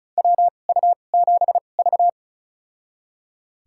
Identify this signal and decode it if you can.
Morse code, heard as a keyed tone, about 36 wpm, spaced as for 25 wpm, 700 Hz. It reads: WU7V
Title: USA Call Signs - 4 Characters - 2x Character Spacing - 35wpm